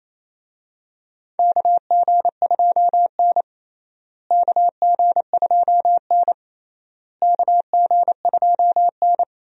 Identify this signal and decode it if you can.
Morse code, heard as a keyed tone, about 28 wpm, 700 Hz. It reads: KG2N KG2N KG2N
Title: USA Call Signs - 4 Characters - Repeated 3x - 28wpm